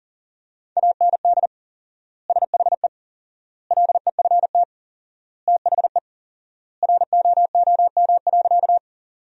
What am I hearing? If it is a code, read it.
Morse code, heard as a keyed tone, about 40 wpm, 700 Hz. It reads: AND SHE LEFT THE ROOM.
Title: 4x Word-Spacing - Sentences from Top 300 Words - 40wpm